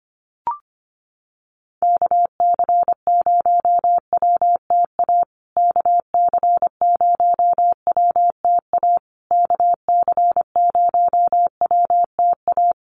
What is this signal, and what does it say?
Morse code, heard as a keyed tone, about 25 wpm, 700 Hz. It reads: KC0WTA KC0WTA KC0WTA
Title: USA Call Signs Repeated 3x 25wpm